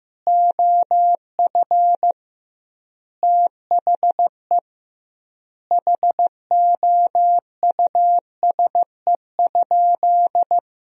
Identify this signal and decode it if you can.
Morse code, heard as a keyed tone, about 15 wpm, 700 Hz. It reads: OF THE HOUSE?
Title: 2x Word-Spacing - Sentences from Top 200 Words - 15wpm